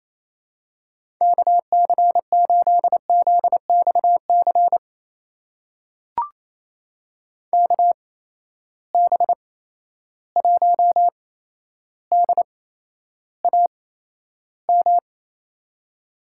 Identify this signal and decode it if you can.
Morse code, heard as a keyed tone, about 28 wpm, spaced as for 12 wpm, 700 Hz. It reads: KC8ZXC KB1DAM
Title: USA Call Signs - 6 Characters - 8x Character Spacing - 28wpm